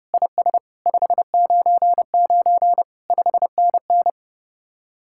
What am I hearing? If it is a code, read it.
Morse code, heard as a keyed tone, about 37 wpm, 700 Hz. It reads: IS 599 5NN
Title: Speed-Racing Common QSO Phrases - 30wpm